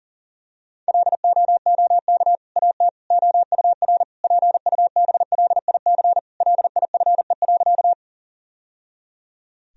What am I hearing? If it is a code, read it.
Morse code, heard as a keyed tone, about 40 wpm, 700 Hz. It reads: LOOK AT OUR PUBLIC LIFE.
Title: Sentences from Top 1000 Words - Review Part 3 - 40wpm